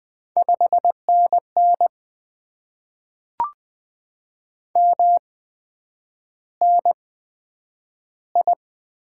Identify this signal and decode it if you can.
Morse code, heard as a keyed tone, about 20 wpm, spaced as for 9 wpm, 700 Hz. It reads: 5NN MNI
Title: Top 90 QSO Elements - 8x Character Spacing - 20wpm